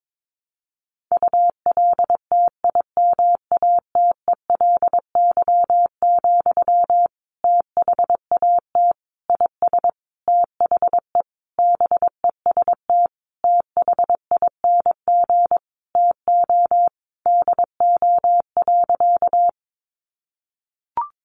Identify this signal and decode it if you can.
Morse code, heard as a keyed tone, about 22 wpm, 700 Hz. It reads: ULTIMATELY, THAT IS THE BEST THING TO DO.
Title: Sentences from Top 1200 Words 22wpm